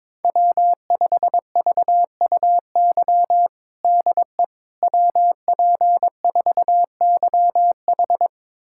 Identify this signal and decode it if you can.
Morse code, heard as a keyed tone, about 22 wpm, 700 Hz. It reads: W5VUY DE WP4YH
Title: USA Callsign DE Callsign - 5 Characters - 22wpm